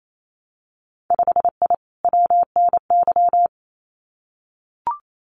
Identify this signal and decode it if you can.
Morse code, heard as a keyed tone, about 28 wpm, 700 Hz. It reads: 5I WNY
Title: Winter Field Day Exchanges Encore 28wpm